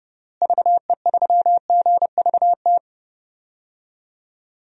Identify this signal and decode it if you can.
Morse code, heard as a keyed tone, about 32 wpm, 700 Hz. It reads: VE3GVT
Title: DXCC-100 Call Signs - 6 Characters - Repeated 3x - 30wpm